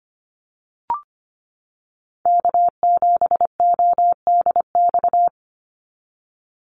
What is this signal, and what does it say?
Morse code, heard as a keyed tone, about 25 wpm, 700 Hz. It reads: K7ODX
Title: NA SOTA Chaser Callsigns 25wpm